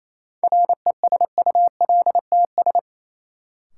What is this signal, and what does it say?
Morse code, heard as a keyed tone, about 28 wpm, 700 Hz. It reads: RESULTS
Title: Sets of 4 Words - New Word Drill - 28wpm